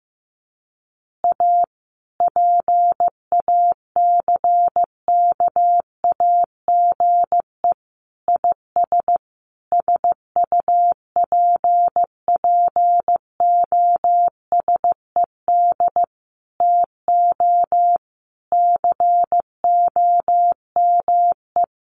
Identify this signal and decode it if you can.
Morse code, heard as a keyed tone, about 15 wpm, 700 Hz. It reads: A PACKAGE IS SUPPOSED TO COME
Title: Easy Phrases 15wpm